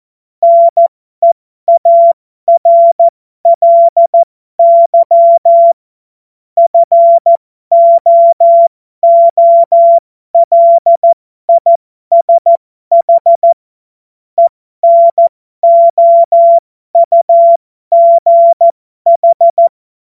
Morse code, heard as a keyed tone, about 14 wpm, spaced as for 12 wpm, 680 Hz. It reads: NEARLY FOOLISH ENOUGH
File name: Morsekurs / Witze